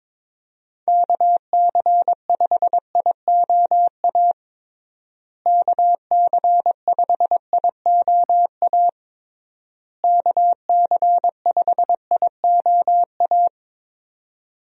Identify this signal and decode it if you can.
Morse code, heard as a keyed tone, about 21 wpm, 700 Hz. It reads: KC5IOA KC5IOA KC5IOA
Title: USA Call Signs - 6 Characters - Repeated 3x - 22wpm